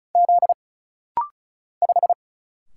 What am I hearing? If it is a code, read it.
Morse code, heard as a keyed tone, about 34 wpm, 700 Hz. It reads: Z 5
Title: Single Letter-Number Alt - Rapid-Fire 35wpm